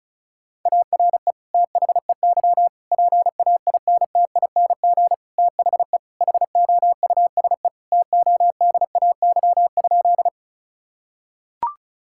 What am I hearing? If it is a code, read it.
Morse code, heard as a keyed tone, about 36 wpm, 700 Hz. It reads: ARE THEY PAINTING THE HOUSE TODAY?
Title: Sentences from Top 1000 Words - Review Part 6 - 35wpm